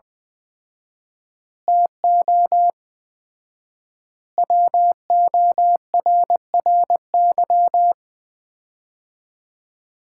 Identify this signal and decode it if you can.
Morse code, heard as a keyed tone, about 20 wpm, 700 Hz. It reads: TO WORRY
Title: 4x Word-Spacing - Sets of 5 Words - 20wpm